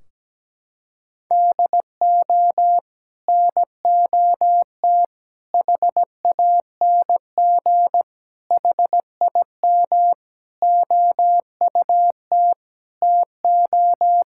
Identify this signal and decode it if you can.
Morse code, heard as a keyed tone, about 17 wpm, 700 Hz. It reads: DO NOT HANG HIM OUT TO
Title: Sentences from Top 1600 Words 17wpm